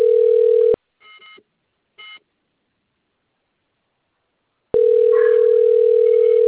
Ringback tone (North America), 440 + 480 Hz.